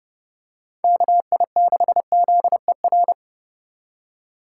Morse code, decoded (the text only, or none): KI6ZER